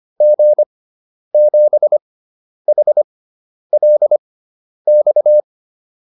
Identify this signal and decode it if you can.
Morse code, heard as a keyed tone, about 25 wpm, spaced as for 10 wpm, 600 Hz. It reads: G7HLX